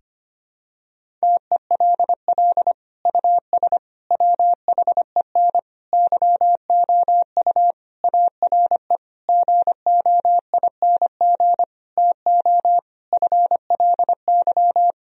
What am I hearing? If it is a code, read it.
Morse code, heard as a keyed tone, about 25 wpm, 700 Hz. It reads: TELL US WHEN YOU ARE GOING TO FLY
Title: Sentences from Top 1000 Words - Review Part 2 - 25wpm